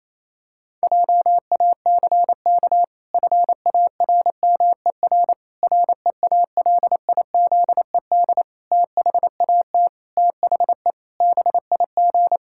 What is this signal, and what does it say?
Morse code, heard as a keyed tone, about 27 wpm, 700 Hz. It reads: JACK FARMER REALIZED THAT THE BIG
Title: Pangram Challenge v2 - 28wpm